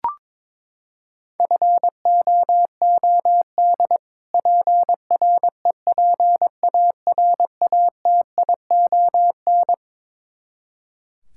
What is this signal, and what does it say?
Morse code, heard as a keyed tone, about 23 wpm, 700 Hz. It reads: FOOD PREPARATION
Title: Practice QSO - Exchange 3 - More Info - 22wpm